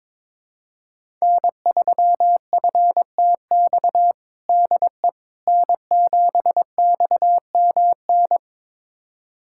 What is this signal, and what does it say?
Morse code, heard as a keyed tone, about 21 wpm, 700 Hz. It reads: N3FTX DE N7XMN
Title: USA Callsign DE Callsign - 5 Characters - 22wpm